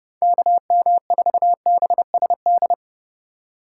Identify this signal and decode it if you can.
Morse code, heard as a keyed tone, about 29 wpm, 700 Hz. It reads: KM4BSD